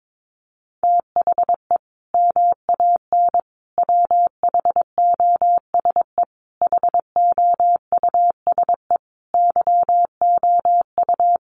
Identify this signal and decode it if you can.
Morse code, heard as a keyed tone, about 22 wpm, 700 Hz. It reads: THE MAN WHOSE HOUSE YOU
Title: Sentences from Top 1200 Words 22wpm